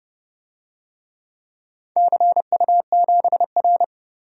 Morse code, heard as a keyed tone, about 31 wpm, 700 Hz. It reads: CU7R
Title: DXCC-100 Call Signs - 4 Characters - 30wpm